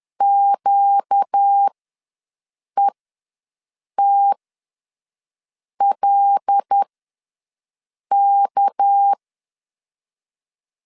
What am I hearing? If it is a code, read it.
Morse code, heard as a keyed tone, about 10 wpm, spaced as for 5 wpm, 800 Hz. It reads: QETLK
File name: LEZIONI / 6 PKX0 / 6 PKX0 ws 10wpm 4220